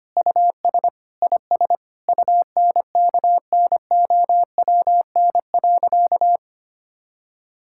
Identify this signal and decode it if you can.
Morse code, heard as a keyed tone, about 25 wpm, 700 Hz. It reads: US IS UNKNOWN.
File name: practice / 25wpm / Sentences with 4-Word Phrases - 25wpm